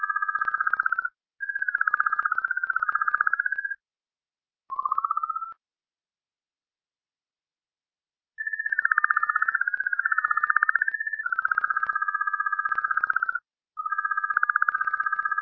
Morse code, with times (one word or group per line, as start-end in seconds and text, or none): none